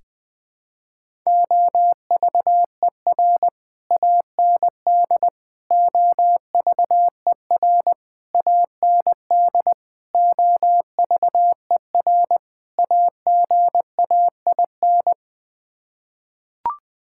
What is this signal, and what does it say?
Morse code, heard as a keyed tone, about 20 wpm, 700 Hz. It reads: OVER AND OVER AND OVER AGAIN
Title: Sets of 6 Words - 20wpm